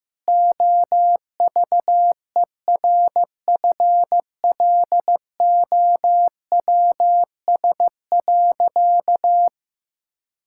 Morse code, heard as a keyed tone, about 16 wpm, 700 Hz. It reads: OVERFLOWS.
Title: Pangram Challenge v2 - 15wpm